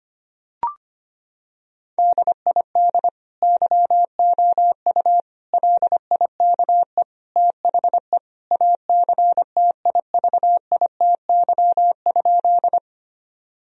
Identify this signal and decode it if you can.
Morse code, heard as a keyed tone, about 25 wpm, 700 Hz. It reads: DID YOU LIKE THE ACTIVITY?
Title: Sentences from Top 1000 Words - Review Part 4 - 25wpm